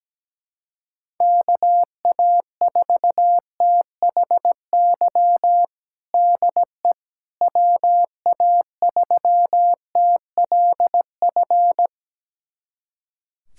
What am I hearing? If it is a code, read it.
Morse code, heard as a keyed tone, about 17 wpm, 700 Hz. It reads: KA4THY DE WA3TLF
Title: USA Callsign DE Callsign 17wpm